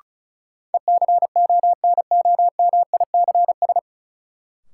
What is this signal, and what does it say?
Morse code, heard as a keyed tone, about 34 wpm, 700 Hz. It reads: ECONOMICS
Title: Sentences from Top 1000 Words - Review Part 3 - New Word Drill - 35wpm